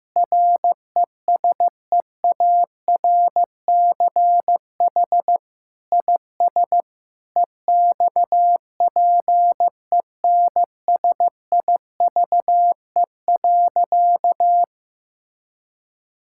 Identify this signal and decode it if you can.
Morse code, heard as a keyed tone, about 15 wpm, 700 Hz. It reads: RESEARCH IS EXPENSIVE.
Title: Sentences from Top 1000 Words - Review Part 6 - 15wpm